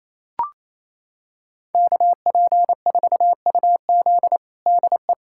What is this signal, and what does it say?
Morse code, heard as a keyed tone, about 27 wpm, 700 Hz. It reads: KP4UZ DE
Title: USA Callsign DE Callsign - 5 Characters - 28wpm